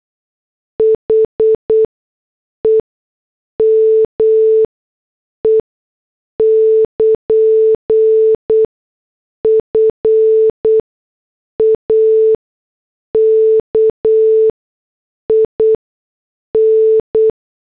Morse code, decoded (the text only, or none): HEME <KN> FAKIN